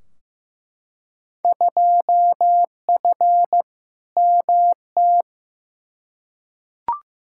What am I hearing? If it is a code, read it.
Morse code, heard as a keyed tone, about 15 wpm, 700 Hz. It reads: 2F MT
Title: Field Day Exchanges Encore - 15wpm